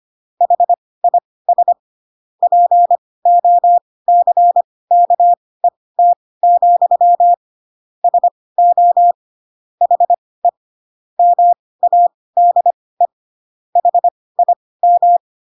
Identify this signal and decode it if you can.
Morse code, heard as a keyed tone, about 25 wpm, spaced as for 18 wpm, 700 Hz. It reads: HIS POCKET, SO HE MADE HIM